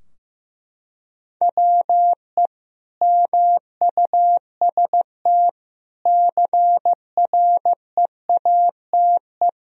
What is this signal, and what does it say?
Morse code, heard as a keyed tone, about 15 wpm, 700 Hz. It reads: WE MUST CREATE
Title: Sentences from Top 1000 Words - Review Part 3 - 15wpm